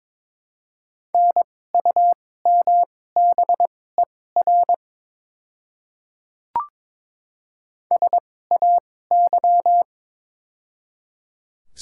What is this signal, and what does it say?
Morse code, heard as a keyed tone, about 21 wpm, spaced as for 16 wpm, 700 Hz. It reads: NUMBER SAY
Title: 2x Character Spacing - Top 200 Words - 22wpm